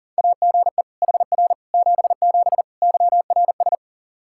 Morse code, heard as a keyed tone, about 37 wpm, 700 Hz. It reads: AGE HR 77 YRS
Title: Practice QSO - Exchange 3 - More Info - 40wpm